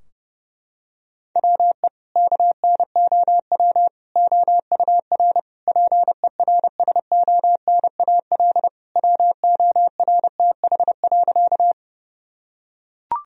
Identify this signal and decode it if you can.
Morse code, heard as a keyed tone, about 30 wpm, 700 Hz. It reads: WE KNOW OUR PERSONAL WORTH.
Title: Sentences from Top 1000 Words - Review Part 4 - 30wpm